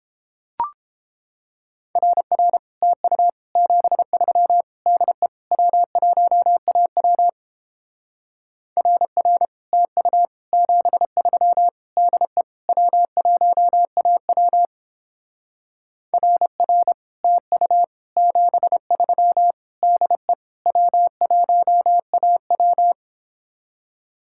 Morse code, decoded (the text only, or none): RR TU 73 DE W1AW RR TU 73 DE W1AW RR TU 73 DE W1AW